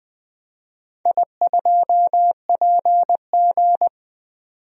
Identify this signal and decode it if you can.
Morse code, heard as a keyed tone, about 20 wpm, 700 Hz. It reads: I2PG
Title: DXCC-100 Call Signs - 4 Characters - 20wpm